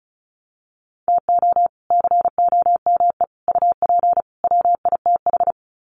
Morse code, heard as a keyed tone, about 34 wpm, 700 Hz. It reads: TO COME UP WITH
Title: Sets of 4 Words 35wpm